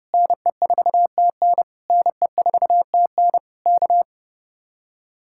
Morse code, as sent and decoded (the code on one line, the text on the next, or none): -. ....- - -. / -. ....- - -. / -.-
NE4TN NE4TN K